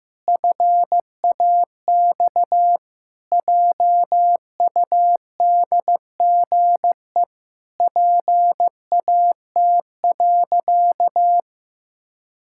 Morse code, heard as a keyed tone, about 15 wpm, 700 Hz. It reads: FAX JUDGE PAT.
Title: Pangram Challenge v2 - 15wpm